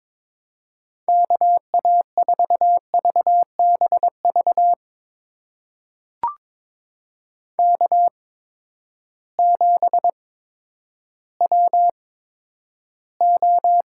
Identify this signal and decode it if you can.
Morse code, heard as a keyed tone, about 21 wpm, spaced as for 11 wpm, 700 Hz. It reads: KA4VBV K7WO
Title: USA Call Signs - 8x Character Spacing - 22wpm